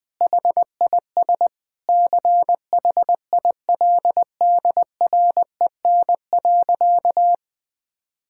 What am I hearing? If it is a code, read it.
Morse code, heard as a keyed tone, about 20 wpm, 700 Hz. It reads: HIS CHILDREN.
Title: Sentences from Top 1600 Words 20wpm